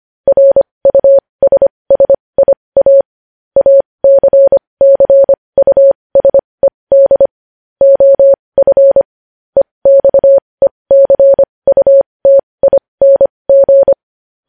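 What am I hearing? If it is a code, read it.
Morse code, heard as a keyed tone, about 25 wpm, spaced as for 20 wpm, 550 Hz. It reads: RUSSIA ACCUSED OF EXECUTING
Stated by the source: Morse code news headlines sent at 20 words per minute.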